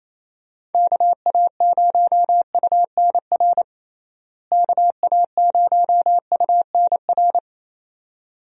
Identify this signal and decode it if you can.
Morse code, heard as a keyed tone, about 28 wpm, 700 Hz. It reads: KA0UNR KA0UNR
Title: USA Call Signs - 6 Characters - Repeated 2x - 28wpm